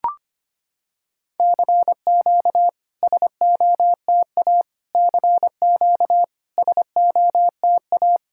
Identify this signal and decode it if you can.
Morse code, heard as a keyed tone, about 25 wpm, 690 Hz. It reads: CQ SOTA CQ SOTA
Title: SOTA QSO Practice 25wpm